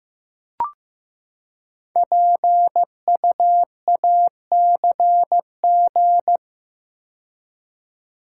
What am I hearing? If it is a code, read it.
Morse code, heard as a keyed tone, about 15 wpm, 700 Hz. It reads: PUACG
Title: Code Groups - 5 Letters - 15wpm